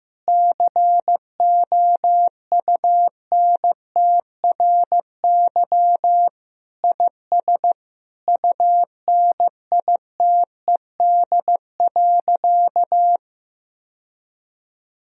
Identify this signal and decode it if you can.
Morse code, heard as a keyed tone, about 15 wpm, 700 Hz. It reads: COUNTRY IS UNITED.
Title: Sentences from Top 1000 Words - Review Part 4 - 15wpm